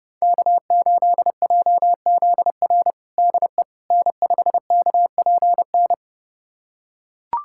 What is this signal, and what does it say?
Morse code, heard as a keyed tone, about 30 wpm, 700 Hz. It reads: K8JZR DE N5KPN
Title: USA Callsign DE Callsign - 5 Characters - 30wpm